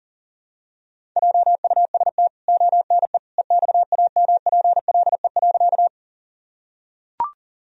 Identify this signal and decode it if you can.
Morse code, heard as a keyed tone, about 40 wpm, 700 Hz. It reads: JUST ONE EXAMPLE.
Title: Sentences from Top 1000 Words - Review Part 1 - 40wpm